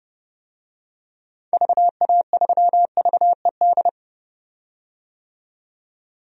Morse code, heard as a keyed tone, about 30 wpm, 700 Hz. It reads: VA3VED
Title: DXCC-100 Call Signs - 6 Characters - 30wpm